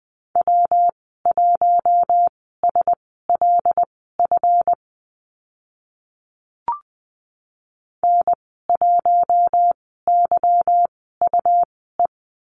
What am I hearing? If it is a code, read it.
Morse code, heard as a keyed tone, about 20 wpm, spaced as for 15 wpm, 700 Hz. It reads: W1SLF N1YUE
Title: 161 Lesson 26g - Letter g - Call Signs 20wpm